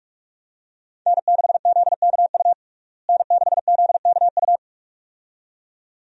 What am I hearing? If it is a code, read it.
Morse code, heard as a keyed tone, about 41 wpm, 700 Hz. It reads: N6ZKU N6ZKU
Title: USA Call Signs - 5 Characters - Repeated 2x - 45wpm